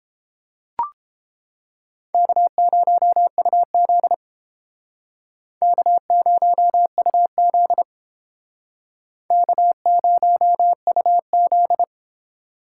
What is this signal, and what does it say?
Morse code, heard as a keyed tone, about 33 wpm, 700 Hz. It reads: K0UZ K0UZ K0UZ